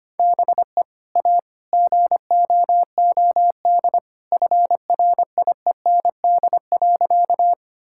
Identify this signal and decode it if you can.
Morse code, heard as a keyed tone, about 25 wpm, 700 Hz. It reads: BE A GOOD FRIEND.
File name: practice / 25wpm / Sentences with 2-Word Phrases - 25wpm